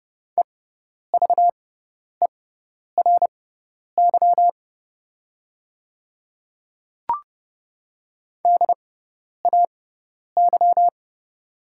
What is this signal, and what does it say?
Morse code, heard as a keyed tone, about 30 wpm, spaced as for 10 wpm, 700 Hz. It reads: EVERY DAY
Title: 6x Character Spacing - Top 200 Words - 30wpm